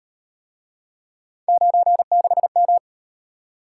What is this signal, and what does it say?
Morse code, heard as a keyed tone, about 36 wpm, 700 Hz. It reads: 96M